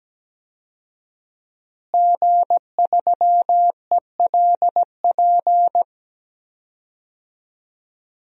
Morse code, heard as a keyed tone, about 17 wpm, 700 Hz. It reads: G3ELP
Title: DXCC-100 Call Signs - 5 Characters - Repeated 3x - 17wpm